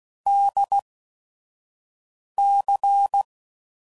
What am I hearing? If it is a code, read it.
Morse code, heard as a keyed tone, about 16 wpm, spaced as for 5 wpm, 800 Hz. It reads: DC
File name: CW Code Groups @5WPM